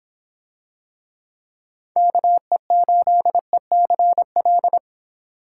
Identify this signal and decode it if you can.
Morse code, heard as a keyed tone, about 26 wpm, 700 Hz. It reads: KE8ECL